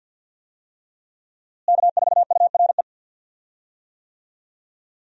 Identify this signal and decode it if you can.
Morse code, heard as a keyed tone, about 52 wpm, 700 Hz. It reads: K4URE
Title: USA Call Signs - 5 Characters - Repeated 3x - 50wpm